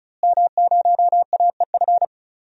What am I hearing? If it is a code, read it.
Morse code, heard as a keyed tone, about 36 wpm, 700 Hz. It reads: M0AEF
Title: DXCC-100 Call Signs - 5 Characters - Repeated 2x - 35wpm